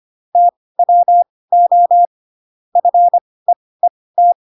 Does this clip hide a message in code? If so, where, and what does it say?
Yes: Morse code from 0.4 to 4.3 s, heard as a keyed tone, about 25 wpm, spaced as for 18 wpm, 700 Hz. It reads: TWO FEET